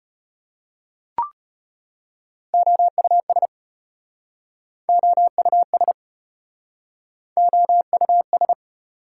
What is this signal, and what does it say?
Morse code, heard as a keyed tone, about 33 wpm, 700 Hz. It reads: OUS OUS OUS